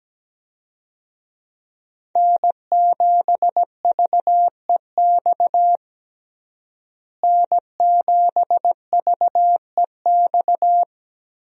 Morse code, heard as a keyed tone, about 17 wpm, 700 Hz. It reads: N7VEX N7VEX
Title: USA Call Signs - 5 Characters - Repeated 2x - 17wpm